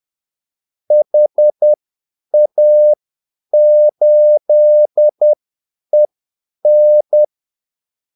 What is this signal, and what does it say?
Morse code, heard as a keyed tone, about 10 wpm, spaced as for 8 wpm, 600 Hz. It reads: HA8EN
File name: Callsigns-10-8wpm